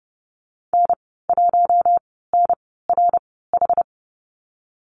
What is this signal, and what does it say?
Morse code, heard as a keyed tone, about 30 wpm, spaced as for 17 wpm, 690 Hz. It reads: N1NRH
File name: Lesson 15g - Number 5 - Call Signs 30wpm